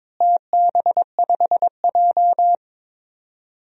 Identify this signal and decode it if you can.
Morse code, heard as a keyed tone, about 22 wpm, 700 Hz. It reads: TB5J
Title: DXCC-100 Call Signs - 4 Characters - Repeated 2x - 22wpm